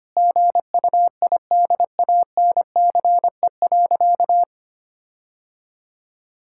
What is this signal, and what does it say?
Morse code, heard as a keyed tone, about 25 wpm, 700 Hz. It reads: GUIDANCE.
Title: Sentences from Top 1600 Words 25wpm